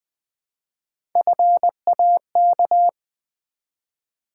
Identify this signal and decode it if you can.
Morse code, heard as a keyed tone, about 20 wpm, 700 Hz. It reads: FAK
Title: Build A Word - 1 Letter At A Time - 4-Letter Words - 20wpm